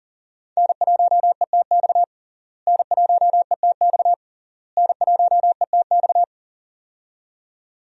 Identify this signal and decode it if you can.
Morse code, heard as a keyed tone, about 40 wpm, 700 Hz. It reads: N1ETX N1ETX N1ETX
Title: USA Call Signs - 5 Characters - Repeated 3x - 40wpm